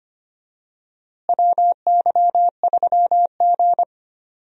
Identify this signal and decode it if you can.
Morse code, heard as a keyed tone, about 25 wpm, 700 Hz. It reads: WY3G
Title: USA Call Signs - 4 Characters - Repeated 3x - 25wpm